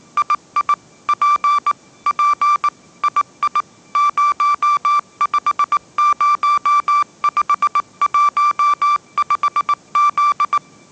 Morse code, heard as a keyed tone, about 19 wpm, 1170 Hz. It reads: II P P II 050515Z
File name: 20wpm